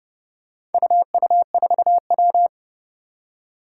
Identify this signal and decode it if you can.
Morse code, heard as a keyed tone, about 30 wpm, 700 Hz. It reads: UU4W
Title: Four Challenging Letter-Number Combinations 30wpm